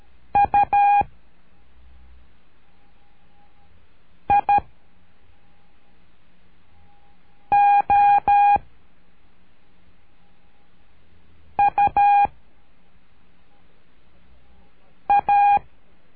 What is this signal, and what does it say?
Morse code, heard as a keyed tone, about 12 wpm, 810 Hz. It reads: U I O U A
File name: cursos / CURSO DE TELEGRAFIA / Telegrafia - Trilha 01